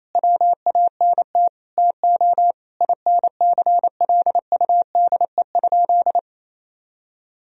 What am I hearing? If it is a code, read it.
Morse code, heard as a keyed tone, about 28 wpm, 700 Hz. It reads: WANT TO INCLUDE?
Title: Sentences from Top 1000 Words - Review Part 3 - 28wpm